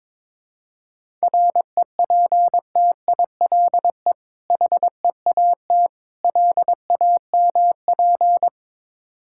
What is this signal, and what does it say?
Morse code, heard as a keyed tone, about 22 wpm, 690 Hz. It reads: REPTILE HEAT LAMP
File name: Difficult Words with Hints - 22wpm